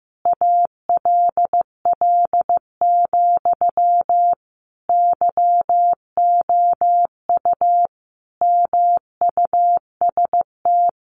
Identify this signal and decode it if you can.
Morse code, heard as a keyed tone, about 15 wpm, 700 Hz. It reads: ALL, YOU MUST
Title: Sentences from Top 1500 Words 15wpm